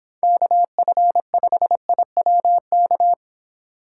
Morse code, heard as a keyed tone, about 25 wpm, 700 Hz. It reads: KF5IWK